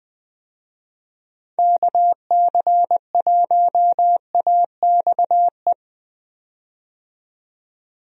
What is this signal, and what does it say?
Morse code, heard as a keyed tone, about 21 wpm, 700 Hz. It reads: KC1AXE
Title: USA Call Signs - 6 Characters - 20wpm